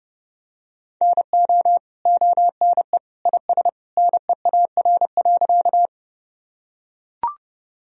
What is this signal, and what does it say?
Morse code, heard as a keyed tone, about 32 wpm, 700 Hz. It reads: NO ONE IS NEAR.
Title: Sentences from Top 1000 Words - Review Part 1 - 30wpm